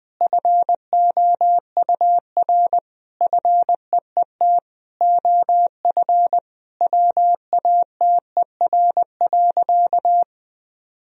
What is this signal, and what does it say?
Morse code, heard as a keyed tone, about 20 wpm, 700 Hz. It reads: FOUR FEET OF WATER.
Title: Sentences from Top 1000 Words - Review Part 1 - 20wpm